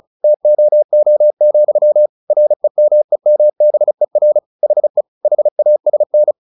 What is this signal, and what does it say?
Morse code, heard as a keyed tone, about 34 wpm, 600 Hz. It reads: TOO, REMEMBER HE HASN
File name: the old man and the sea 2 35wpm